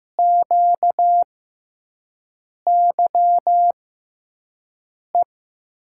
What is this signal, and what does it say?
Morse code, heard as a keyed tone, about 15 wpm, spaced as for 5 wpm, 700 Hz. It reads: QYE
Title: USA Call Signs - 6 Characters - 6x Character Spacing - 15wpm